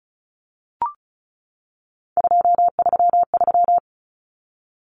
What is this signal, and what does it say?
Morse code, heard as a keyed tone, about 34 wpm, 700 Hz. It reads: 233
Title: Numbers - 3-Digit 35wpm